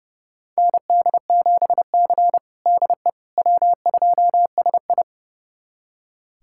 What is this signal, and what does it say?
Morse code, heard as a keyed tone, about 32 wpm, 700 Hz. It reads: ND7C DE W2SI
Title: USA Callsign DE Callsign - 4 Characters - 30wpm